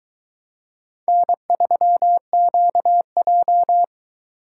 Morse code, heard as a keyed tone, about 22 wpm, 700 Hz. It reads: N3QJ